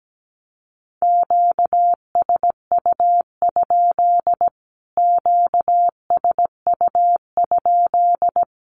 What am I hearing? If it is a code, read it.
Morse code, heard as a keyed tone, about 17 wpm, 700 Hz. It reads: QSU? QSU?
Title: Q-Codes Repeated 3x 17wpm